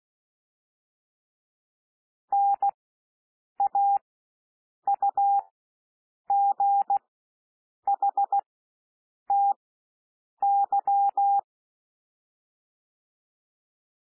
Morse code, heard as a keyed tone, about 16 wpm, spaced as for 7 wpm, 800 Hz. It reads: NAUGHTY
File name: SOUND WORDS 16 wpm